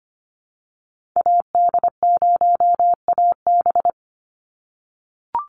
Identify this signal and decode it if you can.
Morse code, heard as a keyed tone, about 25 wpm, 700 Hz. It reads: AD0AB
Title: Call Signs - 25wpm